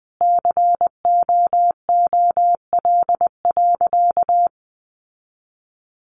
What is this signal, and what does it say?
Morse code, heard as a keyed tone, about 20 wpm, 700 Hz. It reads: COOL.
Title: Sentences from Top 1200 Words 20wpm